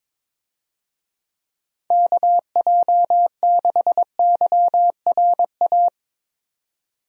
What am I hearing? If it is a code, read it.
Morse code, heard as a keyed tone, about 22 wpm, 700 Hz. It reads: KJ6YRA
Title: USA Call Signs - 6 Characters - 22wpm